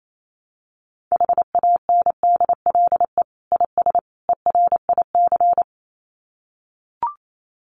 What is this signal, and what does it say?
Morse code, heard as a keyed tone, about 28 wpm, 700 Hz. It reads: HANDLE IS ERIC
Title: Common QSO Phrases 28wpm